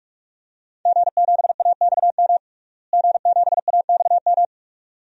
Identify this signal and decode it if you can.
Morse code, heard as a keyed tone, about 41 wpm, 700 Hz. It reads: G7AXM G7AXM
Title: DXCC-100 Call Signs - 5 Characters - Repeated 2x - 45wpm